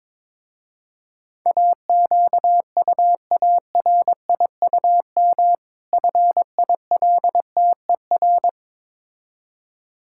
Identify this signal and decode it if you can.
Morse code, heard as a keyed tone, about 21 wpm, 700 Hz. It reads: AQUARIUM FILTER
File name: Difficult Words with Hints - 22wpm